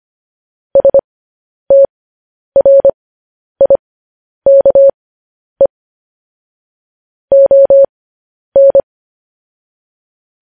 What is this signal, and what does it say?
Morse code, heard as a keyed tone, about 25 wpm, spaced as for 10 wpm, 550 Hz. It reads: STRIKE ON
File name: podcast / News headlines-1x-25x10